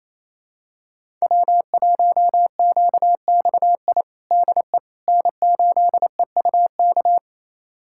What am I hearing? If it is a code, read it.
Morse code, heard as a keyed tone, about 27 wpm, 700 Hz. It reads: W1QXI DE N8EUK
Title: USA Callsign DE Callsign - 5 Characters - 28wpm